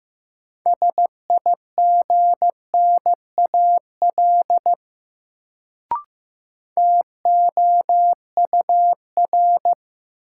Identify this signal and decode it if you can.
Morse code, heard as a keyed tone, about 15 wpm, 700 Hz. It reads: SIGNAL TOUR
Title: Sentences from Top 1400 Words - New Word Drill Alt - 15wpm